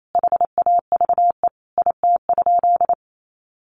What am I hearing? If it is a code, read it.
Morse code, heard as a keyed tone, about 28 wpm, 700 Hz. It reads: HAVE IT?
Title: Sentences from Top 100 Words 28wpm